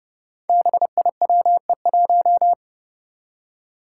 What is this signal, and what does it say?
Morse code, heard as a keyed tone, about 30 wpm, 700 Hz. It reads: BIWE1
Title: Five Challenging Letter-Number Combinations 30wpm